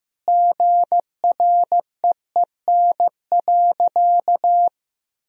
Morse code, heard as a keyed tone, about 15 wpm, 700 Hz. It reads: GREEN.
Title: Sentences from Top 1000 Words - Review Part 2 - 15wpm